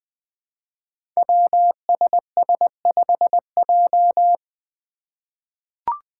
Morse code, heard as a keyed tone, about 20 wpm, 700 Hz. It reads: WSS5J
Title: Five Challenging Letter-Number Combinations 20wpm